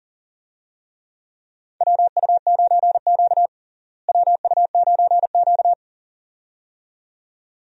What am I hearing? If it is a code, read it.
Morse code, heard as a keyed tone, about 37 wpm, 700 Hz. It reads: WU9Q WU9Q
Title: USA Call Signs - 4 Characters - Repeated 2x - 40wpm